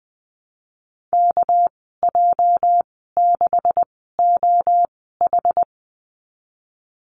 Morse code, heard as a keyed tone, about 20 wpm, spaced as for 15 wpm, 700 Hz. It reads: KJ6OH
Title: 213 Lesson 34g - Number 8 - Call Signs 20wpm